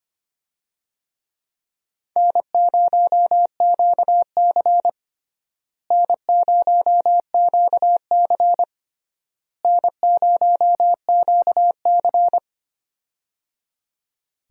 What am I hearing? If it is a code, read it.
Morse code, heard as a keyed tone, about 25 wpm, 700 Hz. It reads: N0QC N0QC N0QC
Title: USA Call Signs - 4 Characters - Repeated 3x - 25wpm